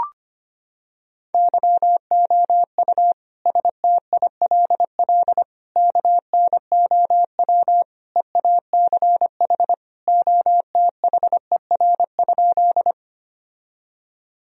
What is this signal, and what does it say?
Morse code, heard as a keyed tone, about 25 wpm, 700 Hz. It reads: YOU STILL KNOW EACH OTHER?
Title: Sentences from Top 1000 Words - Review Part 1 - 25wpm